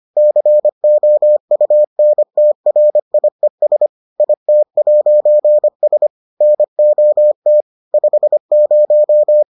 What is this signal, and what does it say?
Morse code, heard as a keyed tone, about 25 wpm, 600 Hz. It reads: COUNTRIES IT'S NOT 50